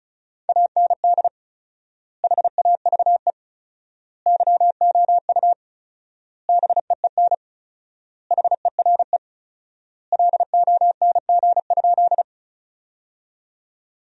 Morse code, heard as a keyed tone, about 36 wpm, 700 Hz. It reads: AND HAVE YOU BEEN HERE LONG?
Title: 4x Word-Spacing - Sentences from Top 200 Words - 35wpm